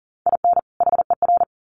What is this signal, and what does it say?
Morse code, heard as a keyed tone, about 40 wpm, 700 Hz. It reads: IN HER
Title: Sets of 2 Words 40wpm